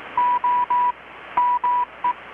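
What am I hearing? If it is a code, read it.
Morse code, heard as a keyed tone, about 18 wpm, 980 Hz. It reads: O ME
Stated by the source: Morse from KPH's Night of Nights XX maritime radiotelegraphy event, received in Chiapas, Mexico about 3,800 km away on a Kenwood R-600 with a grounded 10 m long-wire antenna.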